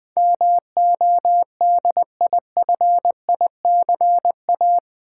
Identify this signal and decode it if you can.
Morse code, heard as a keyed tone, about 20 wpm, 700 Hz. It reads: MODIFICA
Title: Build A Word - 4 Letters At A Time - 12-Letter Words - 20wpm